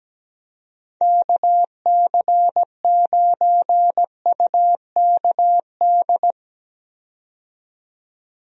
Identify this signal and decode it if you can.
Morse code, heard as a keyed tone, about 17 wpm, 700 Hz. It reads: KC9UKD
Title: USA Call Signs - 6 Characters - 17wpm